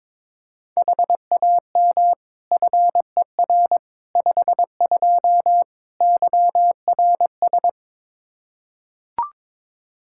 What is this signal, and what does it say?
Morse code, heard as a keyed tone, about 21 wpm, 700 Hz. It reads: HAM FER 52 YRS
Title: Practice QSO - Exchange 3 - More Info - 22wpm